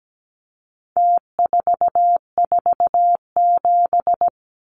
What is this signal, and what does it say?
Morse code, heard as a keyed tone, about 17 wpm, 700 Hz. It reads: T447
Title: Numbers - Contest Style - 4-Digit 17wpm